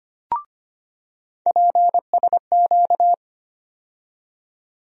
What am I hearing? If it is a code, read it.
Morse code, heard as a keyed tone, about 25 wpm, 700 Hz. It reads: PSQ
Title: USA Call Sign Postfixes - 3 Letters 25wpm